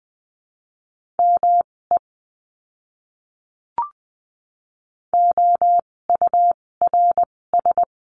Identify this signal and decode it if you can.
Morse code, heard as a keyed tone, about 20 wpm, spaced as for 16 wpm, 700 Hz. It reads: ME OURS
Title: 256 Lesson 42a - Words 12-20wpm